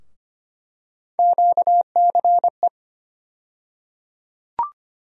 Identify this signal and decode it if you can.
Morse code, heard as a keyed tone, about 25 wpm, 700 Hz. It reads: QCE
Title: USA Call Sign Postfixes - 3 Letters 25wpm